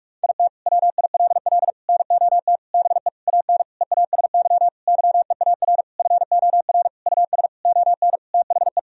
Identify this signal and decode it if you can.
Morse code, heard as a keyed tone, about 46 wpm, 700 Hz. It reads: IT WILL NOT BE AN EASY YEAR FOR US ON THE
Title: Sentences from Top 1000 Words - Review Part 2 - 45wpm